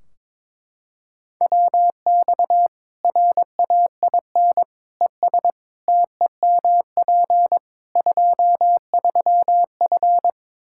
Morse code, heard as a keyed tone, about 23 wpm, 700 Hz. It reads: WX RAIN ES TEMP 23F
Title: Practice QSO - Exchange 2 - Info - 22wpm